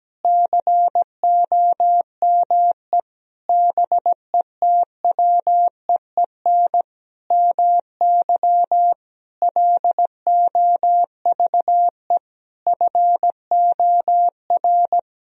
Morse code, heard as a keyed tone, about 17 wpm, 700 Hz. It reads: COME BETWEEN MY LOVE FOR
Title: Sentences from Top 1000 Words - Review Part 1 - 17wpm